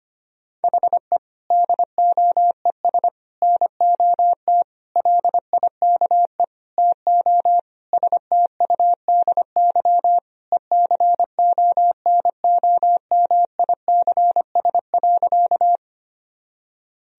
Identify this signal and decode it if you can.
Morse code, heard as a keyed tone, about 25 wpm, 700 Hz. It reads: HE DOES NOT LIKE TO STUDY ECONOMICS.
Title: Sentences from Top 1000 Words - Review Part 3 - 25wpm